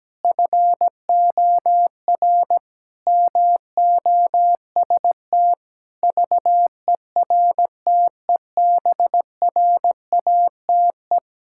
Morse code, heard as a keyed tone, about 17 wpm, 700 Hz. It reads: FOR MOST VERTEBRATE
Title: Pangram Challenge v2 - 17wpm